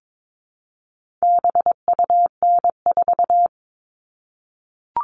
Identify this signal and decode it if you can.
Morse code, heard as a keyed tone, about 22 wpm, 700 Hz. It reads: BUN4
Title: All Cut Numbers - 4-Digit 22wpm